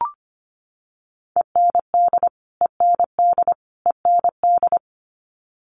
Morse code, heard as a keyed tone, about 25 wpm, 700 Hz. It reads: END END END
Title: Top 500 Words Repeated 25wpm